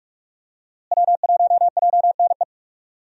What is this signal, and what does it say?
Morse code, heard as a keyed tone, about 45 wpm, 700 Hz. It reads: W1JNE
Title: USA Call Signs - 5 Characters - Repeated 3x - 45wpm